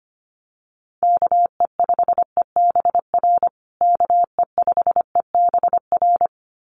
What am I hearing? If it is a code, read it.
Morse code, heard as a keyed tone, about 25 wpm, 700 Hz. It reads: KE5EBR KE5EBR
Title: USA Call Signs Repeated 2x 25wpm